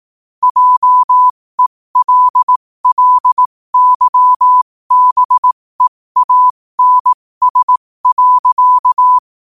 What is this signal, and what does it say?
Morse code, heard as a keyed tone, about 18 wpm, spaced as for 15 wpm, 1000 Hz. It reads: JELLYBEANS.
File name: qotdfiles / 15wpm